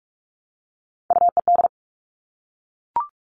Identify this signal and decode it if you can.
Morse code, heard as a keyed tone, about 44 wpm, 700 Hz. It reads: UED
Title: All Cut Numbers - 3-Digit 45wpm